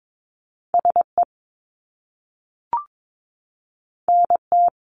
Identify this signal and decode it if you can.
Morse code, heard as a keyed tone, about 22 wpm, 700 Hz. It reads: SE NT
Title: Top Two Letter Combinations in Words 22wpm